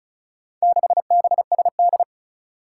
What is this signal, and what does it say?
Morse code, heard as a keyed tone, about 36 wpm, 700 Hz. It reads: 6BSD
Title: Four Challenging Letter-Number Combinations 35wpm